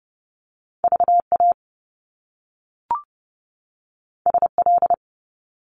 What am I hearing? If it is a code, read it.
Morse code, heard as a keyed tone, about 30 wpm, 700 Hz. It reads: VA SL
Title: Top Two Letter Combinations in Words 30wpm